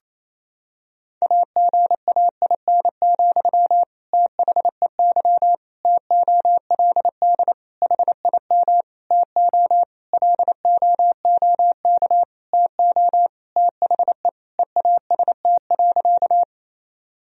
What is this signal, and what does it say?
Morse code, heard as a keyed tone, about 28 wpm, 700 Hz. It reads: AGAIN, THEY TOLD HIM TO LOOK TO THE EAST.
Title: Sentences from Top 1000 Words - Review Part 3 - 28wpm